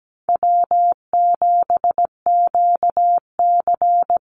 Morse code, heard as a keyed tone, about 17 wpm, 700 Hz. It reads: W7QC
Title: Easy Call Signs Repeated 2x - 17wpm